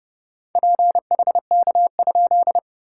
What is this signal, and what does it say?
Morse code, heard as a keyed tone, about 30 wpm, 690 Hz. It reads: PHK?
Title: SOTA QSO Practice 30wpm